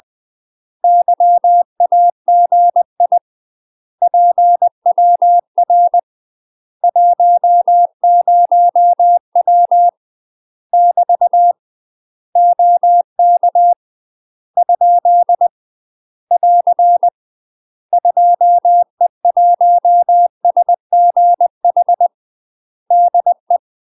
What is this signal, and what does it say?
Morse code, heard as a keyed tone, about 20 wpm, 700 Hz. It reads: YAGI PWR 10W <BT> OK ? <AR> 2E1SGH DE
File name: UK MASTER QSOs TRAINING / 20WPM / UK MASTER QSOs 0014